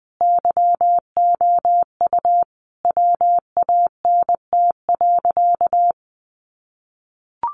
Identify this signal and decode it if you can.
Morse code, heard as a keyed tone, about 20 wpm, 700 Hz. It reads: YOU WANT.
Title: Sentences from Top 200 Words 20wpm